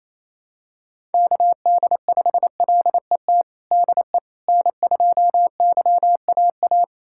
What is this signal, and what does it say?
Morse code, heard as a keyed tone, about 28 wpm, 700 Hz. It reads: KD5LET DE N2YAA
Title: USA Callsign DE Callsign 28wpm